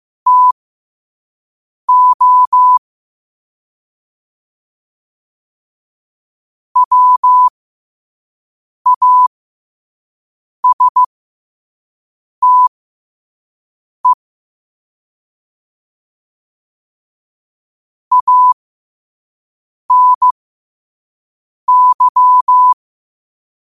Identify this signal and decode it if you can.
Morse code, heard as a keyed tone, about 15 wpm, spaced as for 5 wpm, 1000 Hz. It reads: TO WASTE ANY